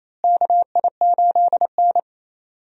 Morse code, heard as a keyed tone, about 28 wpm, 700 Hz. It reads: KI8N
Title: USA Call Signs - 4 Characters - Repeated 3x - 28wpm